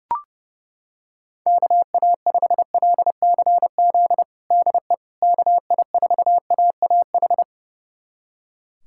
Morse code, heard as a keyed tone, about 29 wpm, 700 Hz. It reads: KA5LCZ DE KI4AAH
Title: USA Callsign DE Callsign - 6 Characters - 30wpm